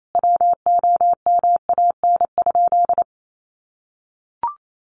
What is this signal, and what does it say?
Morse code, heard as a keyed tone, about 28 wpm, 700 Hz. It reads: WOMAN?
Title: Sentences from Top 1100 Words 28wpm